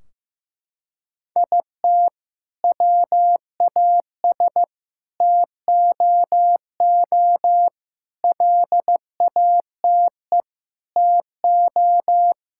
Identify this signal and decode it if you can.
Morse code, heard as a keyed tone, about 15 wpm, 700 Hz. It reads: IT WAS TOO LATE TO
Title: Sentences from Top 1600 Words 15wpm